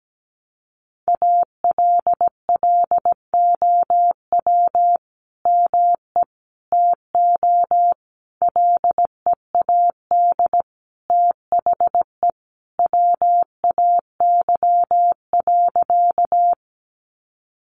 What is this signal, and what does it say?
Morse code, heard as a keyed tone, about 17 wpm, 700 Hz. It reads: ALLOW ME TO LEAD THE WAY.
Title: Sentences from Top 700 Words 17wpm